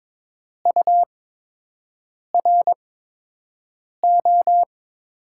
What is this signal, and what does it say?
Morse code, heard as a keyed tone, about 22 wpm, spaced as for 6 wpm, 700 Hz. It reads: URO
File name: Top 90 QSO Elements - 8x Character Spacing - 22wpm